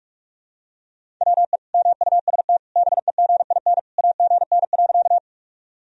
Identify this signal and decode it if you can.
Morse code, heard as a keyed tone, about 46 wpm, 700 Hz. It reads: WE MUST BEGIN AGN.
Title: Sentences from Top 300 Words and Abbreviations 45wpm